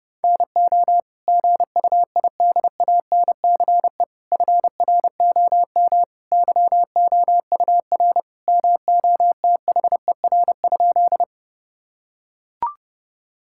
Morse code, heard as a keyed tone, about 30 wpm, 700 Hz. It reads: NO GUIDANCE FROM YOUR MOTHER?
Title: Sentences from Top 1600 Words 30wpm